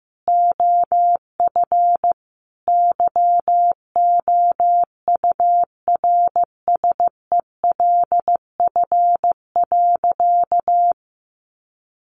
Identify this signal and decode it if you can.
Morse code, heard as a keyed tone, about 15 wpm, 700 Hz. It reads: OF YOURSELF.
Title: Sentences from Top 1400 Words 15wpm